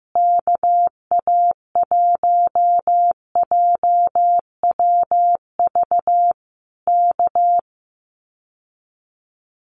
Morse code, heard as a keyed tone, about 15 wpm, 700 Hz. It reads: KA1JWV K
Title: Common QSO Phrases Encore 15wpm